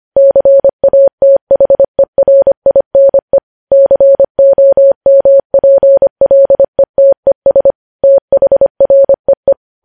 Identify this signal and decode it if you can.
Morse code, heard as a keyed tone, about 25 wpm, 550 Hz. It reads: CATHERINE COMPLETES THREE
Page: news-headlines-in-morse-code-at-25-wpm